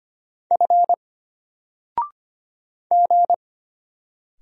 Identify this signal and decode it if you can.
Morse code, heard as a keyed tone, about 25 wpm, 700 Hz. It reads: F G
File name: Single-Letter-Number-25wpm